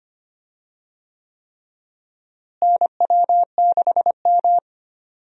Morse code, heard as a keyed tone, about 25 wpm, 700 Hz. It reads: NW6M
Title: USA Call Signs - 4 Characters - Repeated 3x - 25wpm